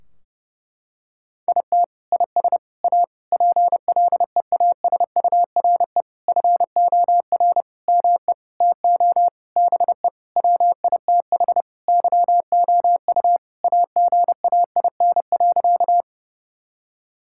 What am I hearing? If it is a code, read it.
Morse code, heard as a keyed tone, about 32 wpm, 700 Hz. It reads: IT IS A PLEASURE FOR ME TO BE WITH YOU AGAIN.
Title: Sentences from Top 1600 Words 30wpm